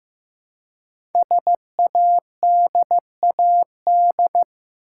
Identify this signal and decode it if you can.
Morse code, heard as a keyed tone, about 15 wpm, 700 Hz. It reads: SADAD
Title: Code Groups - 5 Letters - 15wpm